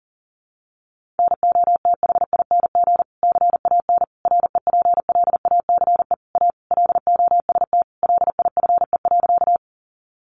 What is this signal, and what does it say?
Morse code, heard as a keyed tone, about 40 wpm, 700 Hz. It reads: NOTHING CAN REPLACE A LOST LIFE.
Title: Sentences from Top 1400 Words 40wpm